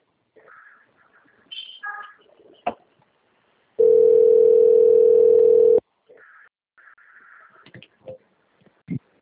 Ringback tone (North America), 440 + 480 Hz.